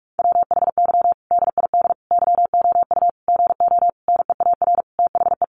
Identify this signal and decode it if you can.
Morse code, heard as a keyed tone, about 44 wpm, 700 Hz. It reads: WHY DID YOU GO NEAR THE
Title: Sentences from Top 1100 Words 45wpm